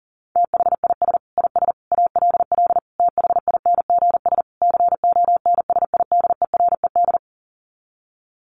Morse code, heard as a keyed tone, about 40 wpm, 700 Hz. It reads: THIS IS ALL THINGS CONSIDERED
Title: Sets of 5 Words 40wpm